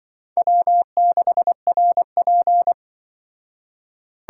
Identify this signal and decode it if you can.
Morse code, heard as a keyed tone, about 25 wpm, 700 Hz. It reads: W6RP
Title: Speed Racing - USA 1x2 Call Signs 20wpm